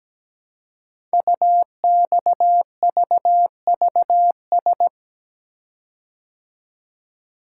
Morse code, heard as a keyed tone, about 17 wpm, 700 Hz. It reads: UXVVS
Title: Code Groups - 5 Letters - 17wpm